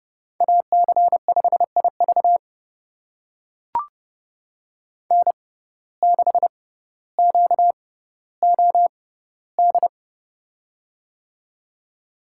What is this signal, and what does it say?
Morse code, heard as a keyed tone, about 31 wpm, spaced as for 15 wpm, 700 Hz. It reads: AC5IV N6QOD
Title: USA Call Signs - 6x Character Spacing - 30wpm